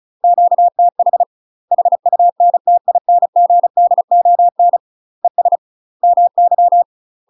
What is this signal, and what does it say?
Morse code, heard as a keyed tone, about 36 wpm, 700 Hz. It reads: QTH HUNTINGDON ES MY
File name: UK MASTER QSOs TRAINING / 35WPM / UK MASTER QSOs 0010